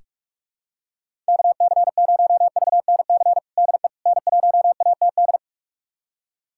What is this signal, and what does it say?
Morse code, heard as a keyed tone, about 44 wpm, 700 Hz. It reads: KC0VNC DE N1ATD
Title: USA Callsign DE Callsign 45wpm